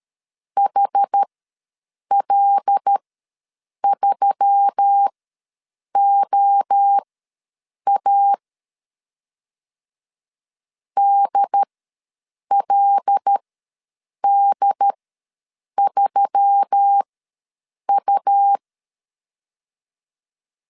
Morse code, heard as a keyed tone, about 13 wpm, spaced as for 7 wpm, 800 Hz. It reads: HL3OA DLD3U
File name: LEZIONI / 5 LBVC / 5 L ws 12wpm 2958